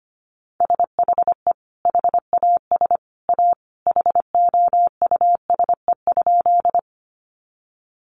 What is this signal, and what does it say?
Morse code, heard as a keyed tone, about 25 wpm, 700 Hz. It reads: SHE HAS A HOUSE?
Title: Sentences from Top 200 Words 25wpm